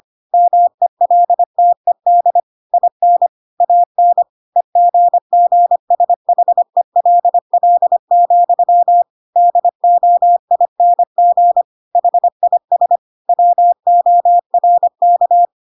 Morse code, heard as a keyed tone, about 25 wpm, 700 Hz. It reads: MELTED IN AN EGGSHELL, DOING HIS WORK
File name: Pinocchio / Pinocchio ch8 25 25 ik1hge